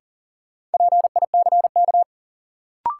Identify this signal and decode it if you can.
Morse code, heard as a keyed tone, about 40 wpm, 700 Hz. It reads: PICK
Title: Sentences from Top 1000 Words - New Word Drill - 40wpm